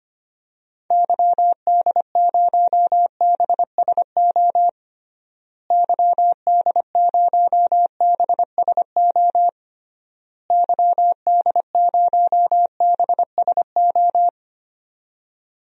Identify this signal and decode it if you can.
Morse code, heard as a keyed tone, about 25 wpm, 700 Hz. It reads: YD0BSO YD0BSO YD0BSO
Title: DXCC-100 Call Signs - 6 Characters - Repeated 3x - 25wpm